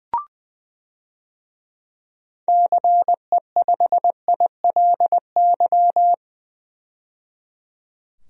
Morse code, heard as a keyed tone, about 20 wpm, 700 Hz. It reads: CE5ILY
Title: DXCC-100 Call Signs - 6 Characters - 20wpm